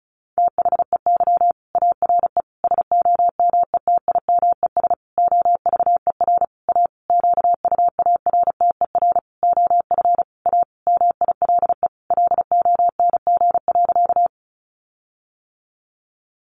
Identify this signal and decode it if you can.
Morse code, heard as a keyed tone, about 34 wpm, 700 Hz. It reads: THEY ARE SOMETIMES OVER A QUARTER OF A MILE LONG.
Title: Sentences from Top 1300 Words 35wpm